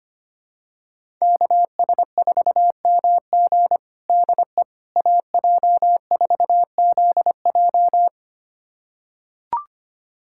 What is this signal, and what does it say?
Morse code, heard as a keyed tone, about 25 wpm, 700 Hz. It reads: KS4MG DE AJ4ZJ
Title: USA Callsign DE Callsign - 5 Characters - 25wpm